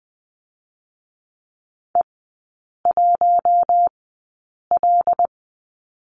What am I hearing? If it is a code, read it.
Morse code, heard as a keyed tone, about 20 wpm, spaced as for 8 wpm, 700 Hz. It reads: E1L
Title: 138 Lesson 23e - Letter f - Three Characters 20wpm